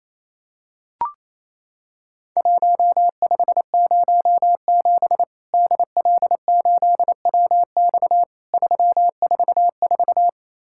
Morse code, heard as a keyed tone, about 28 wpm, 700 Hz. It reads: 1507 DL8WX 344
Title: Worked All Europe DX Contest - QTC Exchange - 28wpm